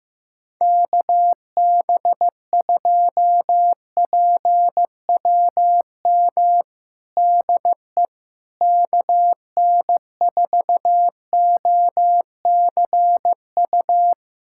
Morse code, heard as a keyed tone, about 16 wpm, 700 Hz. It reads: KB2PWM DE KN4OCU
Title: USA Callsign DE Callsign - 6 Characters - 15wpm